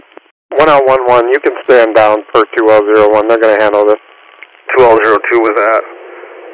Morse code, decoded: none

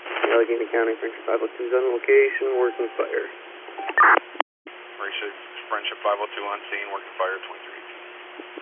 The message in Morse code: none

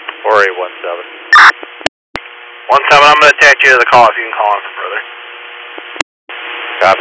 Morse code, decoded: none

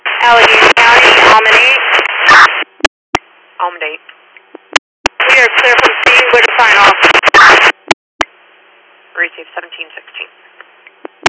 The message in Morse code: none

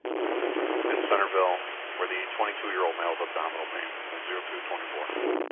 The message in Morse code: none

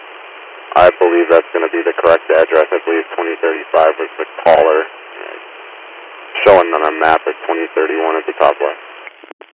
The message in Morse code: none

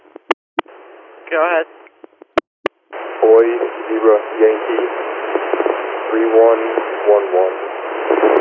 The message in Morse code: none

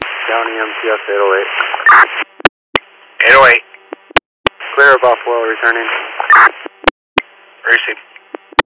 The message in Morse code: none

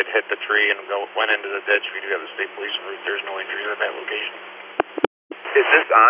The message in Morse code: none

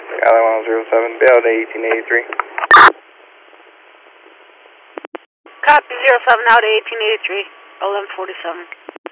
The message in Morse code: none

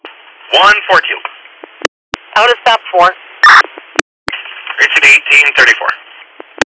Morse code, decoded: none

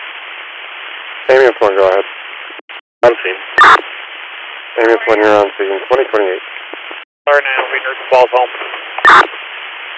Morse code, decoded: none

